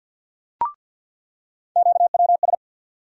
Morse code, heard as a keyed tone, about 48 wpm, 700 Hz. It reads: QWS
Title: USA Call Sign Postfixes - 3 Letters 50wpm